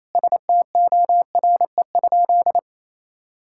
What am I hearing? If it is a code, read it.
Morse code, heard as a keyed tone, about 28 wpm, 700 Hz. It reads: STORE?
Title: Sentences from Top 1000 Words - Review Part 6 - 28wpm